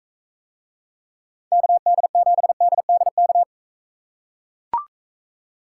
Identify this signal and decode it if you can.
Morse code, heard as a keyed tone, about 43 wpm, 700 Hz. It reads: KD7DDK